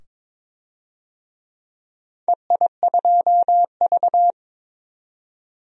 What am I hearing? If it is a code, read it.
Morse code, heard as a keyed tone, about 21 wpm, 700 Hz. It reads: EI2V
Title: DXCC-100 Call Signs - 4 Characters - 22wpm